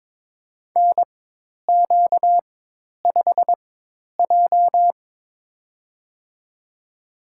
Morse code, heard as a keyed tone, about 21 wpm, spaced as for 10 wpm, 700 Hz. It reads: NQ5J
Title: USA Call Signs - 4 Characters - 4x Character Spacing - 22wpm